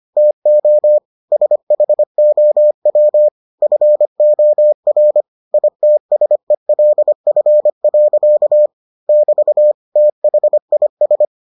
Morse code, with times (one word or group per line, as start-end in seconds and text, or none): 0.2-1.0 TO
1.3-3.3 SHOW
3.6-5.2 FOR
5.5-8.7 ITSELF.
9.1-9.7 <BT>
10.0-11.2 THIS